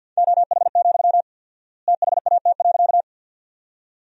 Morse code, heard as a keyed tone, about 50 wpm, 700 Hz. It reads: OH, THAT.
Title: 4x Word-Spacing - Sentences from Top 500 Words - 50wpm